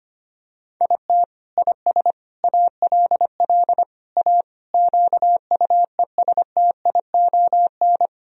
Morse code, heard as a keyed tone, about 25 wpm, 700 Hz. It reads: IT IS ALL A QUESTION
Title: Sentences from Top 1000 Words - Review Part 3 - 25wpm